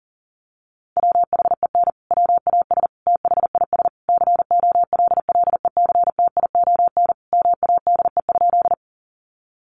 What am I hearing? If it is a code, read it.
Morse code, heard as a keyed tone, about 40 wpm, 700 Hz. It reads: WHEN WAS THIS COLLECTION MADE?
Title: Sentences from Top 1200 Words 40wpm